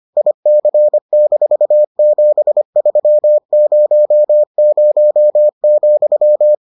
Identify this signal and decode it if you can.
Morse code, heard as a keyed tone, about 25 wpm, 600 Hz. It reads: IC-7300,